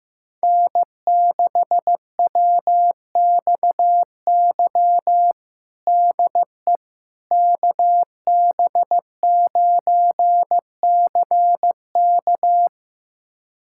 Morse code, heard as a keyed tone, about 15 wpm, 700 Hz. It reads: N6WXY DE KB9CK
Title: USA Callsign DE Callsign - 5 Characters - 15wpm